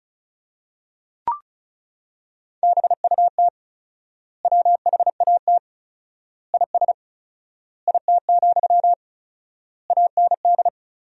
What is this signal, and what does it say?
Morse code, heard as a keyed tone, about 36 wpm, 700 Hz. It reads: BUT WHAT IS IT, AND
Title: 4x Word-Spacing - Sentences from Top 300 Words - 35wpm